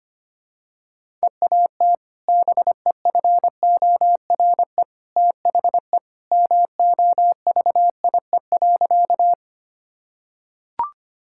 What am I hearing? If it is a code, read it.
Morse code, heard as a keyed tone, about 25 wpm, 700 Hz. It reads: EAT BEFORE THE MOVIE.
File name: practice / 25wpm / Sentences with 2-Word Phrases - 25wpm